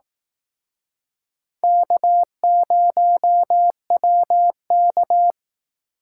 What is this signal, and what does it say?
Morse code, heard as a keyed tone, about 19 wpm, 700 Hz. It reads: K0WK
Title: Speed Racing - USA 1x2 Call Signs 15wpm